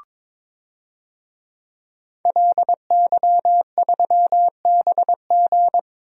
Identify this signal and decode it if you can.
Morse code, heard as a keyed tone, about 22 wpm, 700 Hz. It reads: LY3BG
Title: DXCC-100 Call Signs - 5 Characters - Repeated 2x - 22wpm